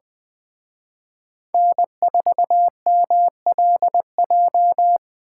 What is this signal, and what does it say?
Morse code, heard as a keyed tone, about 24 wpm, 700 Hz. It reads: N4MLJ
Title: Speed Racing - USA 1x3 Call Signs 20wpm